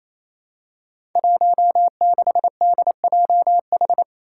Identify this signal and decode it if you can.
Morse code, heard as a keyed tone, about 28 wpm, 700 Hz. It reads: 16DJH
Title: Five Challenging Letter-Number Combinations 28wpm